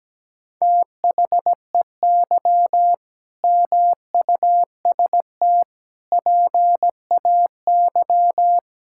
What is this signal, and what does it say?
Morse code, heard as a keyed tone, about 17 wpm, 700 Hz. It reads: THEY MUST PAY
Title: Sentences from Top 1000 Words - Review Part 6 - 17wpm